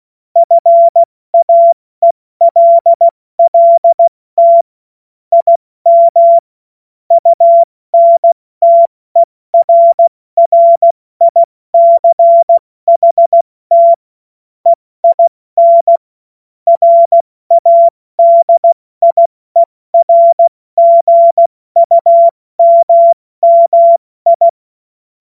Morse code, heard as a keyed tone, about 16 wpm, 680 Hz. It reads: FAELLT IM UNTERRICHT EIN RADIERGUMMI
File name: Morsekurs / Witze / Witz 15. Juni 2021, 70 F80